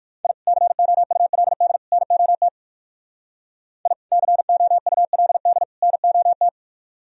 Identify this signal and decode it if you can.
Morse code, heard as a keyed tone, about 56 wpm, 700 Hz. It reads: I COULD NOT I COULD NOT
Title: Speed-Racing - Sets of 3 Words 45wpm